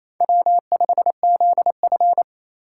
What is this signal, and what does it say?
Morse code, heard as a keyed tone, about 32 wpm, 700 Hz. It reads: W5ZF
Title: Speed Racing - USA 1x2 Call Signs 28wpm